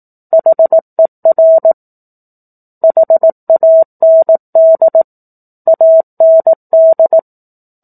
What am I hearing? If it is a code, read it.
Morse code, heard as a keyed tone, about 18 wpm, 650 Hz. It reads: HER HAND AND
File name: 04-Chapter-04 18 WPM